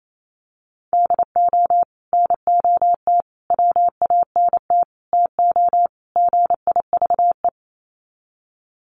Morse code, heard as a keyed tone, about 28 wpm, 700 Hz. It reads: DO NOT WANT TO GIVE
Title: Sets of 5 Words Encore 28wpm